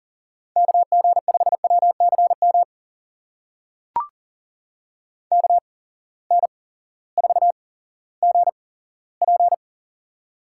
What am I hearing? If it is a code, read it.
Morse code, heard as a keyed tone, about 37 wpm, spaced as for 18 wpm, 700 Hz. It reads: KG5WCM KN4GP
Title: USA Call Signs - 8x Character Spacing - 40wpm